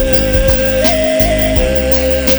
There is no code.